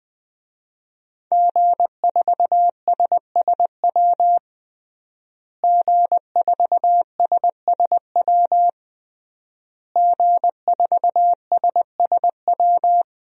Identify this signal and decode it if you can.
Morse code, heard as a keyed tone, about 20 wpm, 700 Hz. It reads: G4SSW G4SSW G4SSW
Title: DXCC-100 Call Signs - 5 Characters - Repeated 3x - 20wpm